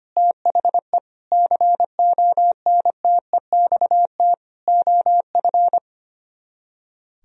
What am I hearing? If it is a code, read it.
Morse code, heard as a keyed tone, about 31 wpm, 700 Hz. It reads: THE CONTEXT OF
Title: Speed-Racing - Sets of 4 Words 25wpm